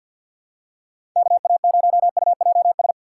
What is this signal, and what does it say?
Morse code, heard as a keyed tone, about 47 wpm, 700 Hz. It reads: KA0UJS
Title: USA Call Signs - 6 Characters - 4x Character Spacing - 50wpm